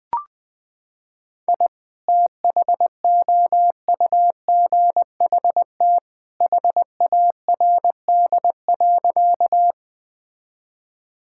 Morse code, heard as a keyed tone, about 20 wpm, 700 Hz. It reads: I THOUGHT HARD.
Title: Sentences with 2-Word Phrases - 20wpm